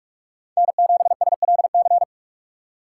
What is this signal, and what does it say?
Morse code, heard as a keyed tone, about 42 wpm, 700 Hz. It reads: N7SLC
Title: USA Call Signs - 5 Characters - 4x Character Spacing - 45wpm